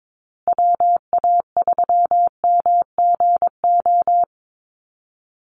Morse code, heard as a keyed tone, about 22 wpm, 700 Hz. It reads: WA3MGO
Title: USA Call Signs Repeated 3x 22wpm